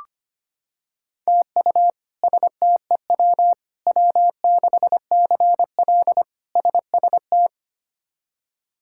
Morse code, heard as a keyed tone, about 25 wpm, 700 Hz. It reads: TU STEW W6CL SST
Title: K1USN SST QSO Practice - 25wpm